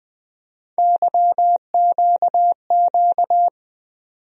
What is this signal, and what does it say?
Morse code, heard as a keyed tone, about 20 wpm, 700 Hz. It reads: YQQ
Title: USA Call Sign Postfixes - 3 Letters 20wpm